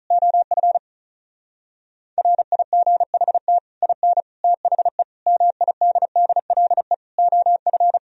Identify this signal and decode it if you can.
Morse code, heard as a keyed tone, about 44 wpm, 700 Hz. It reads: OF RIGHT IN THE MIDDLE OF
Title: Speed-Racing - Sets of 5 Words 35wpm